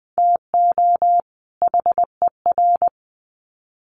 Morse code, heard as a keyed tone, about 19 wpm, 700 Hz. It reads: TO HER
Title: Sets of 2 Words 20wpm